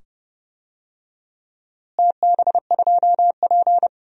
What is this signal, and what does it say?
Morse code, heard as a keyed tone, about 30 wpm, 700 Hz. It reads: TB2P
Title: DXCC-100 Call Signs - 4 Characters - Repeated 2x - 30wpm